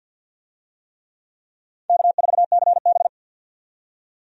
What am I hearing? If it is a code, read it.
Morse code, heard as a keyed tone, about 50 wpm, 700 Hz. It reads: K4CB
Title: USA Call Signs - 4 Characters - 50wpm